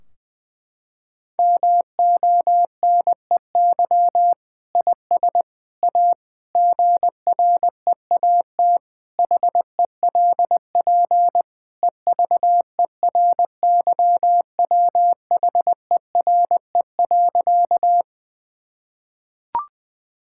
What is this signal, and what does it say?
Morse code, heard as a keyed tone, about 21 wpm, 700 Hz. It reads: MONEY IS A GREAT HELP EVERYWHERE.
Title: Sentences from Top 1600 Words 20wpm